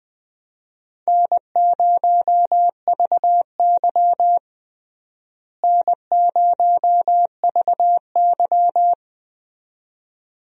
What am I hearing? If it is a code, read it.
Morse code, heard as a keyed tone, about 20 wpm, 700 Hz. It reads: N0VY N0VY
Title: USA Call Signs - 4 Characters - Repeated 2x - 20wpm